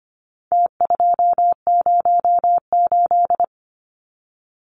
Morse code, heard as a keyed tone, about 25 wpm, 700 Hz. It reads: T208
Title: Numbers - Contest Style - 4-Digit 25wpm